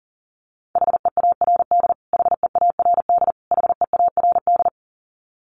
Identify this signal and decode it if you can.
Morse code, heard as a keyed tone, about 40 wpm, 700 Hz. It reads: HEARD HEARD HEARD
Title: Top 500 Words Repeated 40wpm